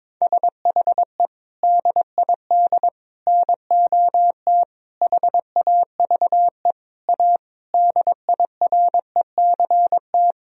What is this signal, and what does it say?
Morse code, heard as a keyed tone, about 22 wpm, 700 Hz. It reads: SHE DID NOT HAVE A DIRECT
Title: Sentences from Top 1000 Words - Review Part 2 - 22wpm